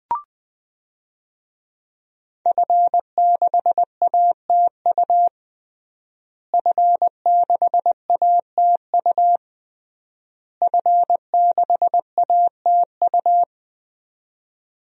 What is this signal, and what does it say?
Morse code, heard as a keyed tone, about 21 wpm, 700 Hz. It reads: F6ATU F6ATU F6ATU
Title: DXCC-100 Call Signs - 5 Characters - Repeated 3x - 20wpm